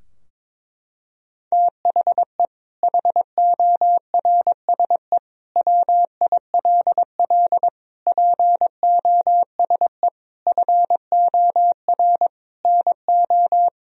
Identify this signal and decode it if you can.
Morse code, heard as a keyed tone, about 22 wpm, 700 Hz. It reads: THE HORSE WILL POSE FOR NO
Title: Sentences from Top 1000 Words - Review Part 2 - 22wpm